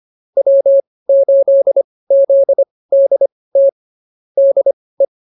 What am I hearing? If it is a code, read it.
Morse code, heard as a keyed tone, about 24 wpm, spaced as for 18 wpm, 550 Hz. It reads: W8ZDT DE